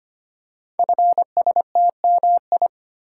Morse code, heard as a keyed tone, about 25 wpm, 700 Hz. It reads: FSTMI
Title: Code Groups - 5 Letters - 25wpm